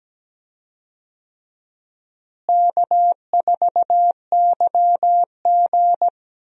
Morse code, heard as a keyed tone, about 17 wpm, 700 Hz. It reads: K4YG
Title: USA Call Signs - 4 Characters - Repeated 2x - 17wpm